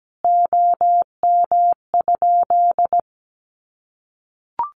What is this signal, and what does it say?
Morse code, heard as a keyed tone, about 17 wpm, 700 Hz. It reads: OM?
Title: Common QSO Phrases 17wpm